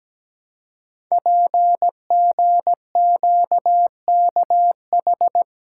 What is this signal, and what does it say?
Morse code, heard as a keyed tone, about 17 wpm, 700 Hz. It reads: PGQKH
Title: Code Groups - 5 Letters - 17wpm